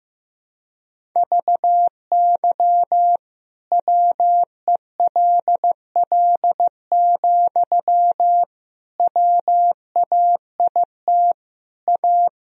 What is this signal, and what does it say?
Morse code, heard as a keyed tone, about 16 wpm, 700 Hz. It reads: VY WELL, WAIT A
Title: Sentences from Top 500 Words and Abbreviations 15wpm